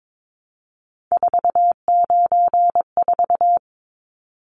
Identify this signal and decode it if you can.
Morse code, heard as a keyed tone, about 22 wpm, 700 Hz. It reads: 494
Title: Numbers - 3-Digit 22wpm